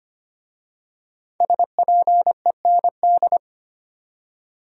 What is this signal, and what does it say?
Morse code, heard as a keyed tone, about 25 wpm, 700 Hz. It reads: SPEND
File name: 5 Letter Words - First 3 Letters Given - 25wpm